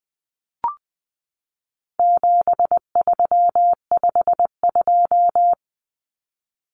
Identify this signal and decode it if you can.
Morse code, heard as a keyed tone, about 20 wpm, 700 Hz. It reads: 7352
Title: Numbers - 4-Digit 20wpm